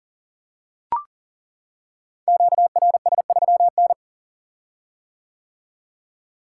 Morse code, heard as a keyed tone, about 42 wpm, 700 Hz. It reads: QRS3N
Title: Code Groups - 5 Letter:Numbers - 40wpm